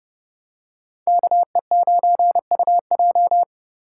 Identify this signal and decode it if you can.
Morse code, heard as a keyed tone, about 28 wpm, 700 Hz. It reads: KE9UJ